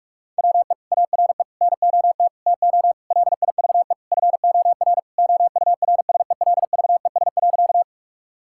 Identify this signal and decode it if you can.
Morse code, heard as a keyed tone, about 48 wpm, 700 Hz. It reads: WE ARE NOT TO LIVE FOR OURSELVES.
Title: Sentences from Top 1600 Words 45wpm